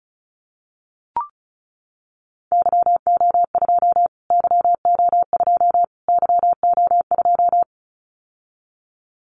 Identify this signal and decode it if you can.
Morse code, heard as a keyed tone, about 34 wpm, 700 Hz. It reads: YO2 YO2 YO2
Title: Callsign Prefixes for Top 25 DXCC - Repeated 3x - 35wpm